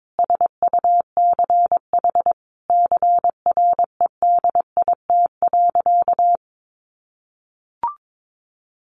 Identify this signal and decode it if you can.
Morse code, heard as a keyed tone, about 22 wpm, 700 Hz. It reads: SUCH CREDIT.
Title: Sentences from Top 1000 Words 22wpm